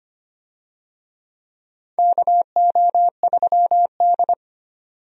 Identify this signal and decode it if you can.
Morse code, heard as a keyed tone, about 25 wpm, 700 Hz. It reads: KO3D
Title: USA Call Signs - 4 Characters - 25wpm